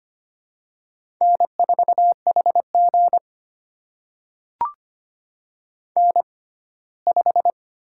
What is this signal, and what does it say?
Morse code, heard as a keyed tone, about 25 wpm, spaced as for 17 wpm, 700 Hz. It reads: N4HG N5
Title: USA Call Signs - 6x Character Spacing - 25wpm